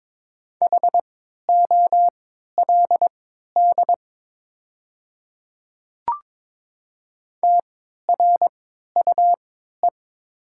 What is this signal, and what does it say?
Morse code, heard as a keyed tone, about 22 wpm, spaced as for 13 wpm, 700 Hz. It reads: HOLD TRUE
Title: 3x Character Spacing - Top 500 Words - 22wpm